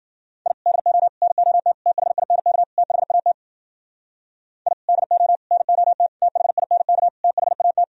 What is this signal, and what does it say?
Morse code, heard as a keyed tone, about 50 wpm, 700 Hz. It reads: I DO NOT THINK THAT I DO NOT THINK THAT
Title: Speed-Racing - Sets of 5 Words 40wpm